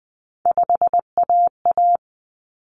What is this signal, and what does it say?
Morse code, heard as a keyed tone, about 20 wpm, 700 Hz. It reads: 5AA
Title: Common Cut Numbers - 3-Digit 20wpm